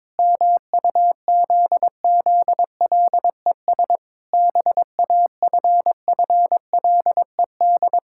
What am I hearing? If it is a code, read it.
Morse code, heard as a keyed tone, about 22 wpm, 700 Hz. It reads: MUZZLES BAFFLED
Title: Pangram Challenge v2 - 22wpm